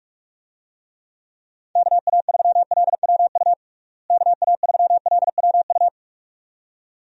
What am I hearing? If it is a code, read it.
Morse code, heard as a keyed tone, about 46 wpm, 700 Hz. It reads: KA3LWU KA3LWU
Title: USA Call Signs - 6 Characters - Repeated 2x - 45wpm